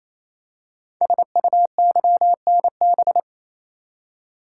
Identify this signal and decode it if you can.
Morse code, heard as a keyed tone, about 27 wpm, 700 Hz. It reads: SUYNB
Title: Code Groups - 5 Letters - 28wpm